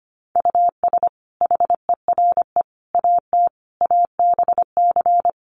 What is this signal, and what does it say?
Morse code, heard as a keyed tone, about 25 wpm, 700 Hz. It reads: US HERE AT ABC
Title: Sets of 5 Words 25wpm